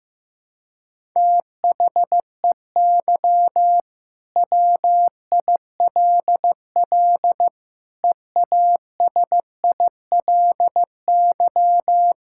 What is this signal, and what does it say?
Morse code, heard as a keyed tone, about 15 wpm, 700 Hz. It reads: THEY WILL EASILY
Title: Sentences from Top 1000 Words - Review Part 6 - 15wpm